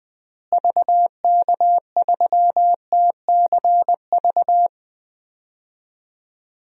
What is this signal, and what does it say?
Morse code, heard as a keyed tone, about 20 wpm, 700 Hz. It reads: VK3TCV
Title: DXCC-100 Call Signs - 6 Characters - Repeated 2x - 20wpm